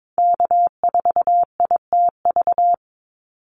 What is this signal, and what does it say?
Morse code, heard as a keyed tone, about 22 wpm, 700 Hz. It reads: K4ITV
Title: Call Signs Repeated 3x - 22wpm